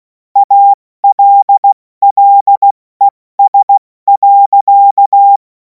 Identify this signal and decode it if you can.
Morse code, heard as a keyed tone, about 16 wpm, 800 Hz. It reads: ALLES.
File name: Morsekurs / Witze / Witz 31. März 2026 70 F80